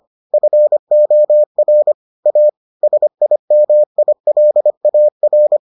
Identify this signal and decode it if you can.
Morse code, heard as a keyed tone, about 25 wpm, 600 Hz. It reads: FOR A SIMILAR